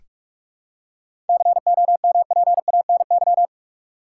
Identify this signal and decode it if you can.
Morse code, heard as a keyed tone, about 46 wpm, 700 Hz. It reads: COMPANY
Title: Seven Letter Words - 45wpm